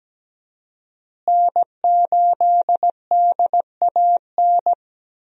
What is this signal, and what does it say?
Morse code, heard as a keyed tone, about 17 wpm, 700 Hz. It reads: N8DAN
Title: USA Call Signs - 5 Characters - Repeated 2x - 17wpm